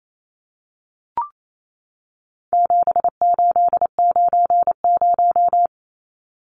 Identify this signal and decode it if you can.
Morse code, heard as a keyed tone, about 28 wpm, 700 Hz. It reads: 7890
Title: Numbers - 4-Digit 28wpm